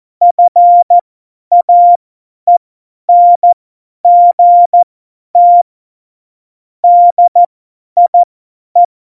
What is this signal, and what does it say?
Morse code, heard as a keyed tone, about 14 wpm, spaced as for 10 wpm, 700 Hz. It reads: FAENGT DIE
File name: Morsekurs / Witze / Witz 05. Januar 2021, 50 F70